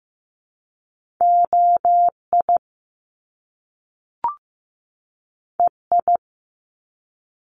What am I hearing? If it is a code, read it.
Morse code, heard as a keyed tone, about 15 wpm, 700 Hz. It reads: OI EI
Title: Top Two Letter Combinations in Words 15wpm